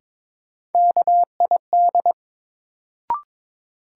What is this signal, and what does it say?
Morse code, heard as a keyed tone, about 22 wpm, 700 Hz. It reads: KID
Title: Sentences from Top 1600 Words - New Word Drill Alt - 22wpm